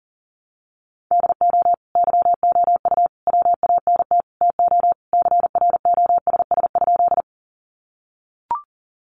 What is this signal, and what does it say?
Morse code, heard as a keyed tone, about 40 wpm, 700 Hz. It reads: DO YOU WANT TO CROSS?
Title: Sentences from Top 300 Words 40wpm